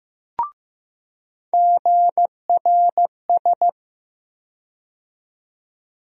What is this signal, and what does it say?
Morse code, heard as a keyed tone, about 15 wpm, 700 Hz. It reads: GRS
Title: USA Call Sign Postfixes - 3 Letters 15wpm